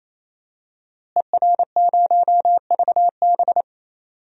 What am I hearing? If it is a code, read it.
Morse code, heard as a keyed tone, about 28 wpm, 700 Hz. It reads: ER0VB
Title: DXCC-100 Call Signs - 5 Characters - Repeated 3x - 28wpm